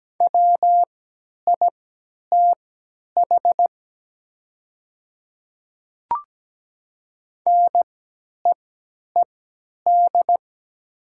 Morse code, heard as a keyed tone, about 17 wpm, spaced as for 10 wpm, 700 Hz. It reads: WITH NEED
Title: 3x Character Spacing - Top 300 Words - 17wpm